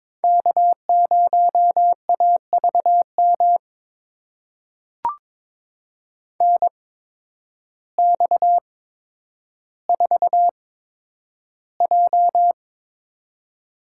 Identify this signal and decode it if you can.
Morse code, heard as a keyed tone, about 23 wpm, spaced as for 10 wpm, 700 Hz. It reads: K0AVM NX4J
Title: USA Call Signs - 8x Character Spacing - 22wpm